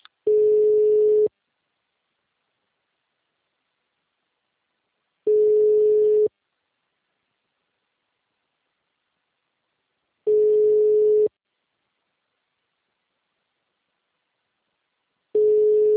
Ringback tone (Europe, 425 Hz).
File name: out- -2012- - - .8